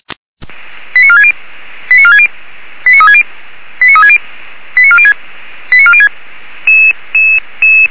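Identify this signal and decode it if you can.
Five-tone selective call (ZVEI), heard as quick runs of five tones, tones of about 70 ms. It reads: 89460 89460 89360 89360 89406 89406
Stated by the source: Five-tone ZVEI selective call received on SDR, fire brigade Feuerwehr Orlamünde.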